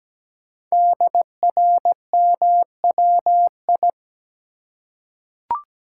Morse code, heard as a keyed tone, about 17 wpm, 700 Hz. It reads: DRMWI
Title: Code Groups - 5 Letters - 17wpm